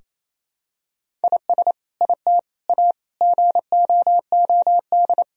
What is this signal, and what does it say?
Morse code, heard as a keyed tone, about 28 wpm, 700 Hz. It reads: IS IT A GOOD
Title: Sentences from Top 1000 Words - Review Part 4 - 28wpm